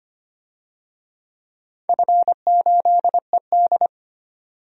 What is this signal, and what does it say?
Morse code, heard as a keyed tone, about 25 wpm, 700 Hz. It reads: F8ED
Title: DXCC-100 Call Signs - 4 Characters - 25wpm